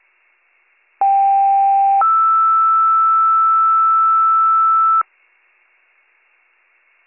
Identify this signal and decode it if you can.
Two-tone page, heard as a one-second tone, then a three-second tone, about 790 then 1320 Hz. It is heard from 1.0 to 5.0 s.